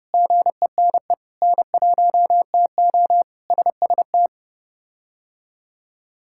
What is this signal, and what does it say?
Morse code, heard as a keyed tone, about 32 wpm, 700 Hz. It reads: GENE N1TO SST
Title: K1USN SST QSO Practice - 30wpm